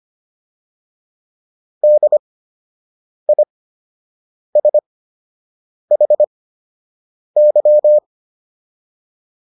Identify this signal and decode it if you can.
Morse code, heard as a keyed tone, about 25 wpm, spaced as for 7 wpm, 600 Hz. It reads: DISHY